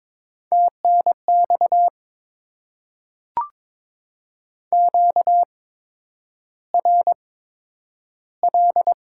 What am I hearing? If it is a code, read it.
Morse code, heard as a keyed tone, about 23 wpm, spaced as for 10 wpm, 700 Hz. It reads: TNX QRL
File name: Top 20 QSO Elements - 8x Character Spacing - 22wpm